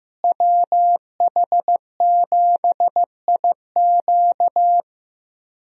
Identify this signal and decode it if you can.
Morse code, heard as a keyed tone, about 15 wpm, 700 Hz. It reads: WH7IQ
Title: USA Call Signs - 5 Characters - Repeated 3x - 15wpm